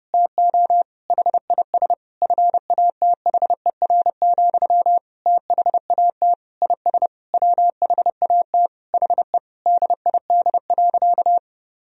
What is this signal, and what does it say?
Morse code, heard as a keyed tone, about 30 wpm, 700 Hz. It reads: TO HIS FATHER, THAT IS WHAT HE DID.
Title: Sentences from Top 1000 Words - Review Part 4 - 30wpm